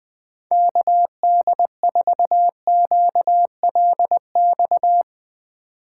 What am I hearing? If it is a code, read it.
Morse code, heard as a keyed tone, about 20 wpm, 700 Hz. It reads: KD4QLX
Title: Speed Racing - USA 2x3 Call Signs 17wpm